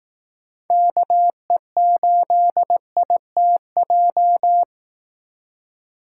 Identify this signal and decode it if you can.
Morse code, heard as a keyed tone, about 18 wpm, 700 Hz. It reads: KE8ITJ
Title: Speed Racing - USA 2x3 Call Signs 15wpm